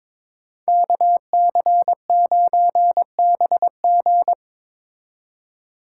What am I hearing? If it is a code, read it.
Morse code, heard as a keyed tone, about 22 wpm, 700 Hz. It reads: KC9BG
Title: K1USN SST QSO Practice - 22wpm